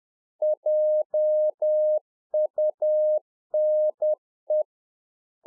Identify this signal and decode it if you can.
Morse code, heard as a keyed tone, about 10 wpm, 610 Hz. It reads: JUNE
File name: 2011-04-mp3 / UNFOGETTABLE QSO 10WPM